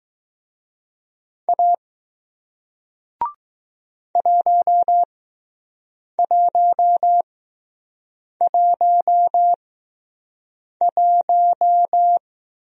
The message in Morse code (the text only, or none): A 1111